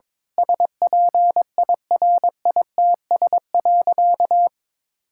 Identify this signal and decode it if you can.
Morse code, heard as a keyed tone, about 22 wpm, 700 Hz. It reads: SPIRITS.
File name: Difficult Sentences with Hints - 22wpm